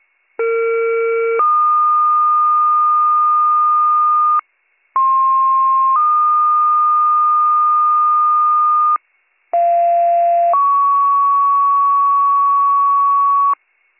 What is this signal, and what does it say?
Two-tone page, heard as a one-second tone, then a three-second tone, about 460 then 1220 Hz, about 1060 then 1220 Hz, about 690 then 1120 Hz. It is heard from 0.4 to 13.5 s.